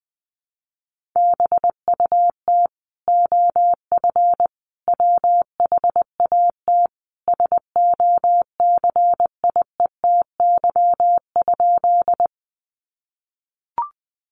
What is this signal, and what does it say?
Morse code, heard as a keyed tone, about 20 wpm, 700 Hz. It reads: BUT OF WHAT SOCIETY?
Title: Sentences from Top 700 Words 20wpm